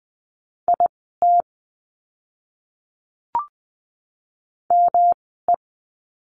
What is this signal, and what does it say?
Morse code, heard as a keyed tone, about 20 wpm, spaced as for 15 wpm, 700 Hz. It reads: IT ME
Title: 160 Lesson 26f - Letter g - Words 20wpm